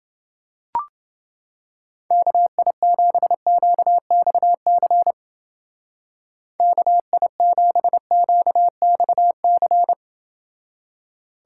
Morse code, heard as a keyed tone, about 28 wpm, 700 Hz. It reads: KI7QXC KI7QXC